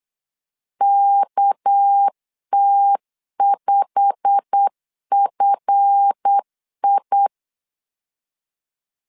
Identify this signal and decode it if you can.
Morse code, heard as a keyed tone, about 8 wpm, 800 Hz. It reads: KT5FI
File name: LEZIONI / 8 YJ45 / 8 YJ45 8wpm 3214